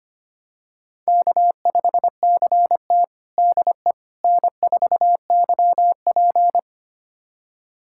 Morse code, heard as a keyed tone, about 25 wpm, 700 Hz. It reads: K5CT DE N4YP
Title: USA Callsign DE Callsign - 4 Characters - 25wpm